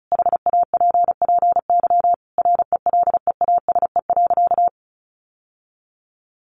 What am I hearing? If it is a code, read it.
Morse code, heard as a keyed tone, about 34 wpm, 700 Hz. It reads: HAPPY RELEASE.
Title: Sentences from Top 1100 Words 35wpm